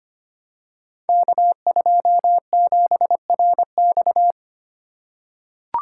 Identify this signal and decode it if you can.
Morse code, heard as a keyed tone, about 25 wpm, 700 Hz. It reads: K27RX
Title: Code Groups - 5 Letter:Numbers - 25wpm